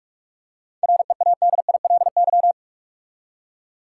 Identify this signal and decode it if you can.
Morse code, heard as a keyed tone, about 46 wpm, 700 Hz. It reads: READILY
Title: Seven Letter Words - 45wpm